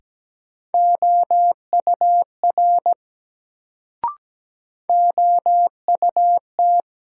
Morse code, heard as a keyed tone, about 17 wpm, 700 Hz. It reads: OUR OUT
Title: Sets of 2 Words - New Word Drill - 17wpm